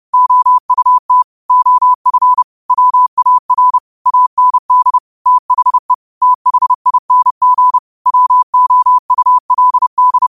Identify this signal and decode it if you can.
Morse code, heard as a keyed tone, about 30 wpm, 1000 Hz. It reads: OUT OF WAR AND THE THING WOULD